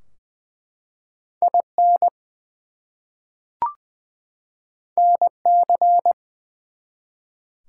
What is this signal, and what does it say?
Morse code, heard as a keyed tone, about 20 wpm, 700 Hz. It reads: IN NC
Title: US States and Canadian Provinces 20wpm